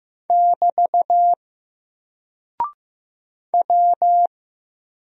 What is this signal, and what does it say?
Morse code, heard as a keyed tone, about 15 wpm, 700 Hz. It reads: <BT> W
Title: Single Character (Letter:Number:Punctuation:Prosign) Alt 15wpm